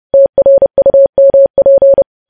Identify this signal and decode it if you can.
Morse code, heard as a keyed tone, about 30 wpm, 550 Hz. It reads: TRUMP
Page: news headlines in morse code at 30 wpm